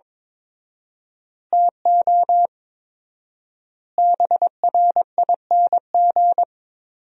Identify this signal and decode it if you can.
Morse code, heard as a keyed tone, about 22 wpm, 700 Hz. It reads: TO BRING
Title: 4x Word-Spacing - Sentences from Top 500 Words - 22wpm